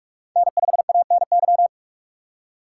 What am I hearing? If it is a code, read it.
Morse code, heard as a keyed tone, about 47 wpm, 700 Hz. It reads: N5ANY
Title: USA Call Signs - 5 Characters - 3x Character Spacing - 45wpm